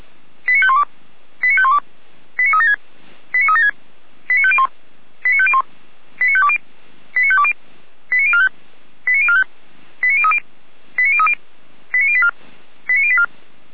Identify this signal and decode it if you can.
Five-tone selective call (ZVEI), heard as quick runs of five tones, tones of about 70 ms. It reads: 89512 89512 89376 89376 89501 89501 89520 89520 89045 89045 89030 89030 89084 89084